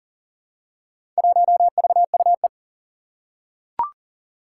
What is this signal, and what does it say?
Morse code, heard as a keyed tone, about 39 wpm, 700 Hz. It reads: 1VUE
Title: Four Challenging Letter-Number Combinations 40wpm